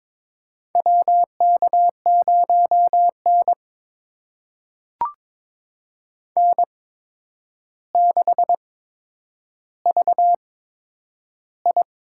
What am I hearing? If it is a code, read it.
Morse code, heard as a keyed tone, about 22 wpm, spaced as for 9 wpm, 700 Hz. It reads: WK0N N6VI
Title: USA Call Signs - 8x Character Spacing - 22wpm